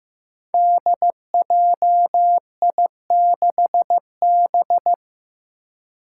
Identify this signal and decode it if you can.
Morse code, heard as a keyed tone, about 15 wpm, 700 Hz. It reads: DJI6B
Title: Five Challenging Letter-Number Combinations 15wpm